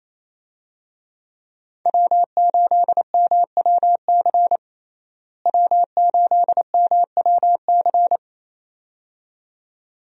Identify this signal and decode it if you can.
Morse code, heard as a keyed tone, about 27 wpm, 700 Hz. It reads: W8MWC W8MWC
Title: USA Call Signs - 5 Characters - Repeated 2x - 28wpm